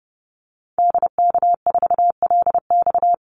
Morse code, heard as a keyed tone, about 30 wpm, 700 Hz. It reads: DK4LX
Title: Call Signs - 30wpm